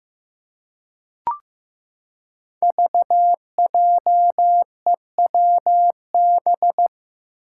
Morse code, heard as a keyed tone, about 15 wpm, 700 Hz. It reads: VJEWB
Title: Five Challenging Letter-Number Combinations 15wpm